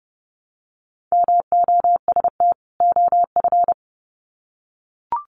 Morse code, heard as a keyed tone, about 30 wpm, 700 Hz. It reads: MOST OF
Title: Sets of 2 Words 30wpm